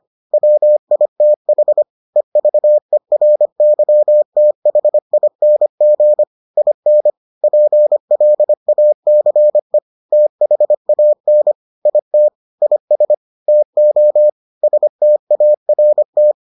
Morse code, heard as a keyed tone, about 25 wpm, 600 Hz. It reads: WITH EVERYTHING IN PLACE THAN IT IS TO START